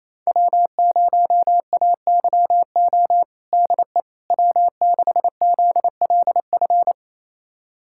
Morse code, heard as a keyed tone, about 28 wpm, 700 Hz. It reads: W0AYO DE W6ZLF
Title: USA Callsign DE Callsign - 5 Characters - 28wpm